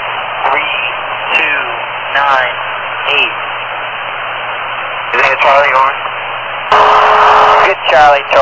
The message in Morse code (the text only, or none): none